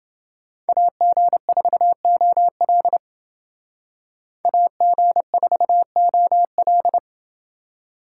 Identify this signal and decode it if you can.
Morse code, heard as a keyed tone, about 29 wpm, 700 Hz. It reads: AG4OL AG4OL